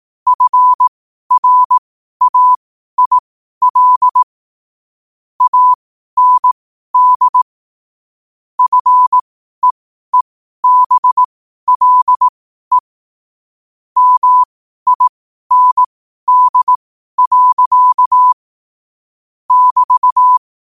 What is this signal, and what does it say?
Morse code, heard as a keyed tone, about 18 wpm, spaced as for 12 wpm, 1000 Hz. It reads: FRAIL AND FEEBLE MIND. <BT>
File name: qotdfiles / 12wpm